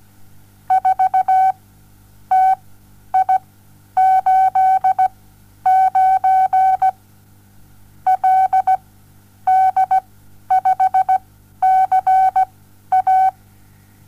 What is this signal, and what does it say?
Morse code, heard as a keyed tone, about 17 wpm, spaced as for 10 wpm, 760 Hz. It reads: 4TI89 LD5CA
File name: Onlinelehrgang / morsen / DJ4UF Morsen07